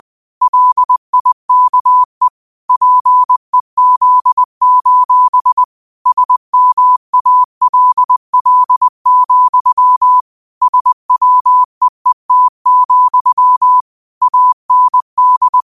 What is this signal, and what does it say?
Morse code, heard as a keyed tone, about 20 wpm, 1000 Hz. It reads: LIKE PEZ: SMALL, SWEET, AND